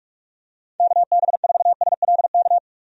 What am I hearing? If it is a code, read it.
Morse code, heard as a keyed tone, about 46 wpm, 700 Hz. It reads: KB4SLK
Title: USA Call Signs - 6 Characters - 45wpm